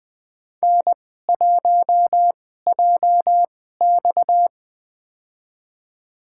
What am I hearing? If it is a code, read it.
Morse code, heard as a keyed tone, about 20 wpm, spaced as for 15 wpm, 700 Hz. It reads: N1JX
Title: USA Call Signs - 4 Characters - 2x Character Spacing - 20wpm